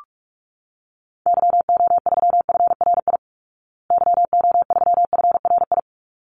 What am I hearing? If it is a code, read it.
Morse code, heard as a keyed tone, about 44 wpm, 700 Hz. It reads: YO3FRI YO3FRI
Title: Call Signs Repeated 3x - 45wpm